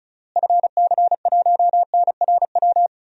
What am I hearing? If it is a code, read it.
Morse code, heard as a keyed tone, about 36 wpm, 700 Hz. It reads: FC1NRW
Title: DXCC-100 Call Signs - 6 Characters - Repeated 3x - 35wpm